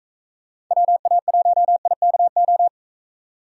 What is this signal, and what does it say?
Morse code, heard as a keyed tone, about 45 wpm, 700 Hz. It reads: WA1IKO